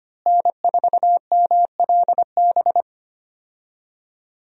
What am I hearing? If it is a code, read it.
Morse code, heard as a keyed tone, about 25 wpm, 700 Hz. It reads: N4MLB
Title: USA Call Signs - 5 Characters - Repeated 3x - 25wpm